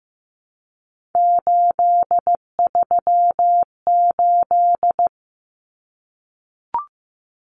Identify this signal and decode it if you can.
Morse code, heard as a keyed tone, about 15 wpm, 700 Hz. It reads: 838
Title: Numbers - 3-Digit 15wpm